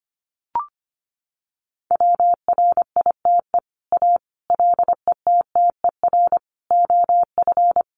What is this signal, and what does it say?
Morse code, heard as a keyed tone, about 25 wpm, 700 Hz. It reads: WRITE A LETTER OF
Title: Sentences from Top 1400 Words 25wpm